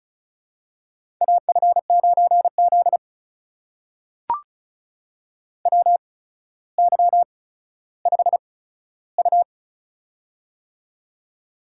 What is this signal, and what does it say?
Morse code, heard as a keyed tone, about 36 wpm, spaced as for 15 wpm, 700 Hz. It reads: AF9Z WY5U
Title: USA Call Signs - 4 Characters - 8x Character Spacing - 35wpm